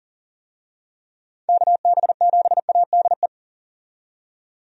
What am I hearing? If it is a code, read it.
Morse code, heard as a keyed tone, about 38 wpm, 700 Hz. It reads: KB7ADE
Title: USA Call Signs - 6 Characters - 40wpm